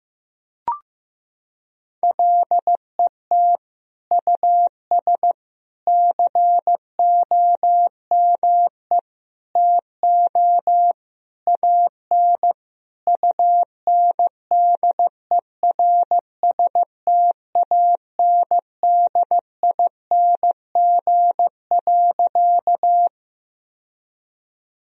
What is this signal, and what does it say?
Morse code, heard as a keyed tone, about 15 wpm, 700 Hz. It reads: LET US COME TO AN UNDERSTANDING.
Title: Sentences from Top 1000 Words - Review Part 3 - 15wpm